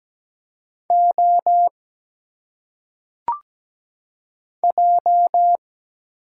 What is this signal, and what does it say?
Morse code, heard as a keyed tone, about 17 wpm, 700 Hz. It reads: O J
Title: Single Letters - Phonetic Pronunciation 17wpm